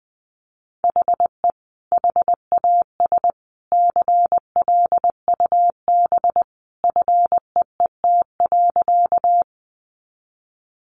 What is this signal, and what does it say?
Morse code, heard as a keyed tone, about 20 wpm, 700 Hz. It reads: HE HAS CLUB FEET.
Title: Sentences from Top 1200 Words 20wpm